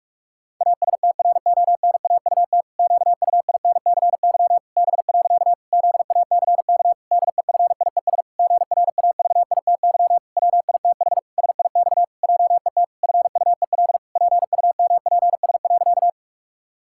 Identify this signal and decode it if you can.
Morse code, heard as a keyed tone, about 46 wpm, 700 Hz. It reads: ASTRONAUT QUINCY B. ZACK DEFIES GRAVITY WITH SIX JET FUEL PUMPS.
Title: Pangram Challenge v2 - 45wpm